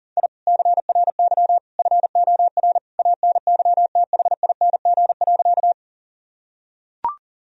Morse code, heard as a keyed tone, about 40 wpm, 700 Hz. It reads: I CRY FOR ANYTHING.
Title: Sentences from Top 1000 Words - Review Part 2 - 40wpm